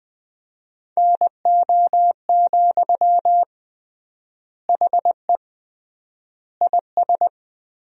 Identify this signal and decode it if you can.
Morse code, heard as a keyed tone, about 20 wpm, 700 Hz. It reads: NO, HE IS
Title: 3x Word-Spacing - Sentences from Top 400 Words - 20wpm